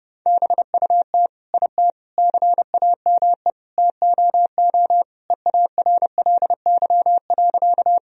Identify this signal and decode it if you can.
Morse code, heard as a keyed tone, about 30 wpm, 700 Hz. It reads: BUT IT CAME TOO EARLY.
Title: Sentences from Top 1000 Words - Review Part 2 - 30wpm